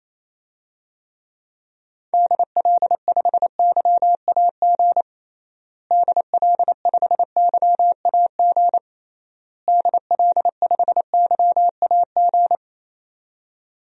Morse code, heard as a keyed tone, about 28 wpm, 700 Hz. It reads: DL5YAG DL5YAG DL5YAG
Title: DXCC-100 Call Signs - 6 Characters - Repeated 3x - 28wpm